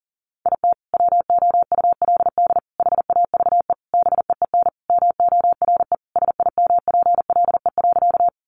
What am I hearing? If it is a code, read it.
Morse code, heard as a keyed tone, about 40 wpm, 700 Hz. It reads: IT WOULD HAVE BEEN MORE SIMPLE.
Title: Sentences from Top 400 Words 40wpm